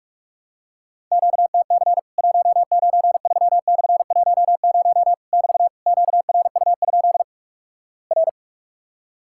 Morse code, heard as a keyed tone, about 46 wpm, 700 Hz. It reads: QTC 193/10 <BT> QRU? R
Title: Worked All Europe DX Contest - QTC Exchange - 45wpm